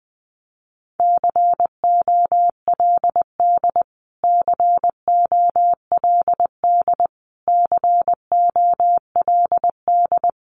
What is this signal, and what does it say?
Morse code, heard as a keyed tone, about 20 wpm, 700 Hz. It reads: COLD COLD COLD
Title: Top 500 Words Repeated 20wpm